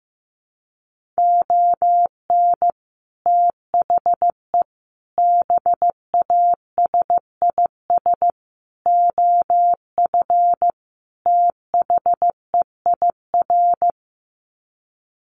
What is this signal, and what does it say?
Morse code, heard as a keyed tone, about 15 wpm, 700 Hz. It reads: ON THE BASIS OF THEIR
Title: Sets of 5 Words Encore 15wpm